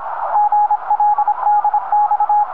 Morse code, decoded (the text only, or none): GL DX